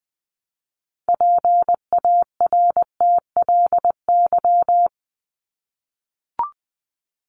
Morse code, heard as a keyed tone, about 20 wpm, 700 Hz. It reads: PARTLY
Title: LY Suffix - Top 100 Words with Base Words 20wpm